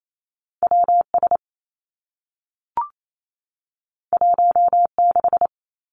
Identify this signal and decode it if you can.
Morse code, heard as a keyed tone, about 28 wpm, 700 Hz. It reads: W S 1 6
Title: Two Challenging Letter-Number Combinations 28wpm